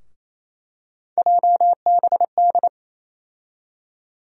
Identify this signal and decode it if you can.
Morse code, heard as a keyed tone, about 28 wpm, 700 Hz. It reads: JBD
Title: USA Call Sign Postfixes - 3 Letters 28wpm